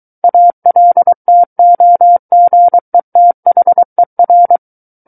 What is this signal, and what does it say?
Morse code, heard as a keyed tone, about 23 wpm, 700 Hz. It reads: ALTOGETHER
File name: Book One / 14-Chapter 14 - 23 WPM